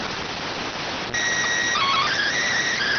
SSTV (slow-scan television), PD 290.